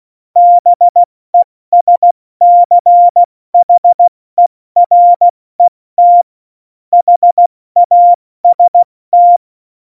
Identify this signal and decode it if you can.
Morse code, heard as a keyed tone, about 16 wpm, 700 Hz. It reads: BESCHERET HAST
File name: Morsekurs / Witze / Witz 09. Oktober 2018, 70 F80